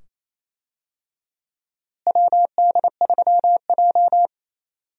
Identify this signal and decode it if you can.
Morse code, heard as a keyed tone, about 27 wpm, 700 Hz. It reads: WD3J
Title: USA Call Signs - 4 Characters - Repeated 2x - 28wpm